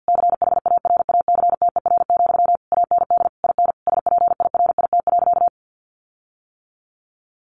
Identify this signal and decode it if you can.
Morse code, heard as a keyed tone, about 50 wpm, 700 Hz. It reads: CHARACTER, AND IN SPIRIT.
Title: Sentences from Top 1400 Words 50wpm